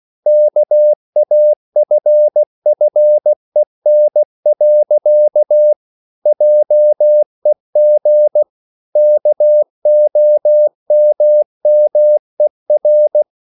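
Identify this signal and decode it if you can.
Morse code, heard as a keyed tone, about 16 wpm, 600 Hz. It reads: KAFFEN. JEG KOMMER
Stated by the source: Slow (QRS) Morse code at 16 wpm, from CW med Gnister.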